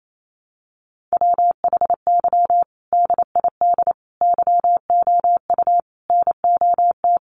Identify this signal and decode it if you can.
Morse code, heard as a keyed tone, about 28 wpm, 700 Hz. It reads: WHY DID YOU NOT
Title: Sentences from Top 1200 Words 28wpm